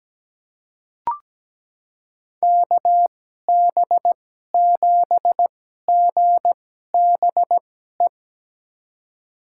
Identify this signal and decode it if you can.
Morse code, heard as a keyed tone, about 17 wpm, spaced as for 12 wpm, 700 Hz. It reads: KB7GBE
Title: USA Call Signs - 2x Character Spacing - 17wpm